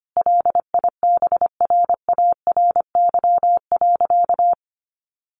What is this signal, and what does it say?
Morse code, heard as a keyed tone, about 25 wpm, 700 Hz. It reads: LIBRARY.
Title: Sentences from Top 600 Words 25wpm